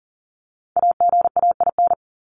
Morse code, heard as a keyed tone, about 40 wpm, 700 Hz. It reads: AGAIN
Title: Top 200 Words 40wpm